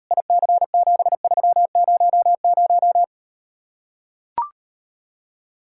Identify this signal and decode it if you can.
Morse code, heard as a keyed tone, about 38 wpm, 700 Hz. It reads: IC7300